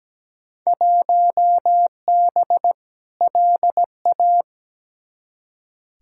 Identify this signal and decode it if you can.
Morse code, heard as a keyed tone, about 17 wpm, 700 Hz. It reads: 1B LA
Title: Field Day Exchanges Encore - 17wpm